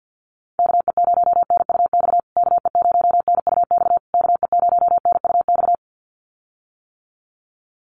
Morse code, heard as a keyed tone, about 48 wpm, 700 Hz. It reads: KE0NUX KE0NUX KE0NUX
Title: USA Call Signs Repeated 3x 50wpm